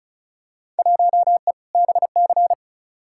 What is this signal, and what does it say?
Morse code, heard as a keyed tone, about 36 wpm, 700 Hz. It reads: 1E BC
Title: Field Day Exchanges - 35wpm